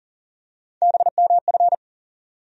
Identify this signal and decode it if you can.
Morse code, heard as a keyed tone, about 40 wpm, 700 Hz. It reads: BMF
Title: USA Call Sign Postfixes - 3 Letters 40wpm